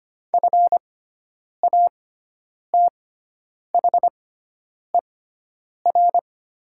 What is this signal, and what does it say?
Morse code, heard as a keyed tone, about 25 wpm, spaced as for 9 wpm, 700 Hz. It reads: FATHER
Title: 6x Character Spacing - Top 300 Words - 25wpm